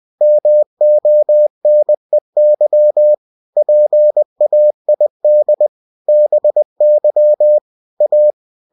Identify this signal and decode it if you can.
Morse code, heard as a keyed tone, about 20 wpm, 600 Hz. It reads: MONEY PAID BY A
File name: cw-books / winnie-the-pooh / winnie-20wpm-page-0117